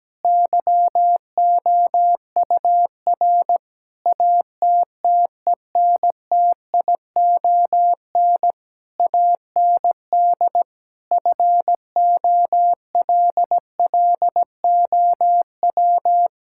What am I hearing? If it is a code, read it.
Morse code, heard as a keyed tone, about 17 wpm, 700 Hz. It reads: YOUR ATTENTION AND FOLLOW
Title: Sentences from Top 1000 Words - Review Part 6 - 17wpm